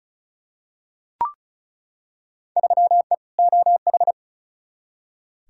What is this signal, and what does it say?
Morse code, heard as a keyed tone, about 36 wpm, 700 Hz. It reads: 3E OH
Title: Field Day Exchanges - 35wpm